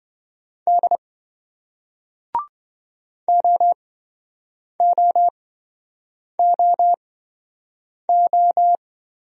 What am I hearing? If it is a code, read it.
Morse code, heard as a keyed tone, about 27 wpm, spaced as for 7 wpm, 700 Hz. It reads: D OOOO